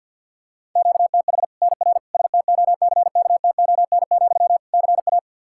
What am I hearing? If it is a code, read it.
Morse code, heard as a keyed tone, about 52 wpm, 700 Hz. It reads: QTH NR STOCKTON, CA
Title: Practice QSO - Exchange 1 - Beginning - 50wpm